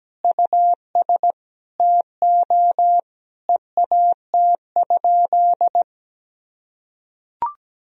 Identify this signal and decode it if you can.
Morse code, heard as a keyed tone, about 17 wpm, 700 Hz. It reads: US TO EAT?
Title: Sentences from Top 1000 Words - Review Part 3 - 17wpm